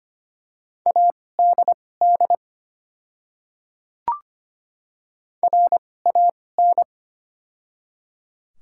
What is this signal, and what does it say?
Morse code, heard as a keyed tone, about 25 wpm, spaced as for 18 wpm, 690 Hz. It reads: ADD RAN
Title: 2x Character Spacing - Top 500 Words - 25wpm